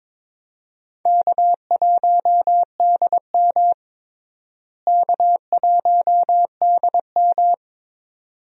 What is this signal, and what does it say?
Morse code, heard as a keyed tone, about 22 wpm, 700 Hz. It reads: K1DM K1DM
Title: USA Call Signs - 4 Characters - Repeated 2x - 22wpm